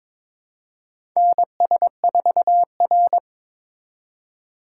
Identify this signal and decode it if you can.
Morse code, heard as a keyed tone, about 22 wpm, 700 Hz. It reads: NS4R
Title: USA Call Signs - 4 Characters - 3x Character Spacing - 22wpm